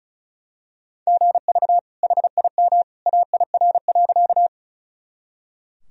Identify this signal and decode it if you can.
Morse code, heard as a keyed tone, about 36 wpm, 700 Hz. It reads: GV HIM AIR.
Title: Sentences from Top 200 Words and Abbreviations 35wpm